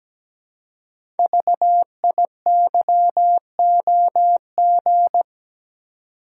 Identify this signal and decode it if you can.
Morse code, heard as a keyed tone, about 17 wpm, 700 Hz. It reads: VIYOG
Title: Code Groups - 5 Letters - 17wpm